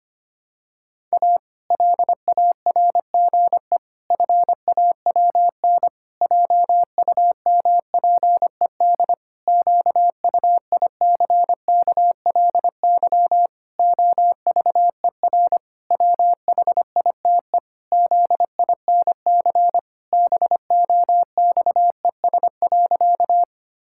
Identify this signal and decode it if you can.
Morse code, heard as a keyed tone, about 25 wpm, 700 Hz. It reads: A LARGE FAWN JUMPED QUICKLY OVER WHITE ZINC BOXES.
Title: Pangram Challenge v2 - 25wpm